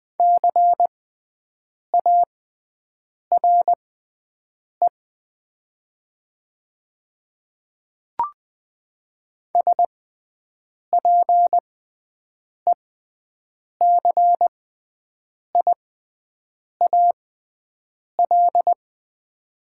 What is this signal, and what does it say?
Morse code, heard as a keyed tone, about 21 wpm, spaced as for 7 wpm, 700 Hz. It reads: CARE SPECIAL
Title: 6x Character Spacing - Top 500 Words - 20wpm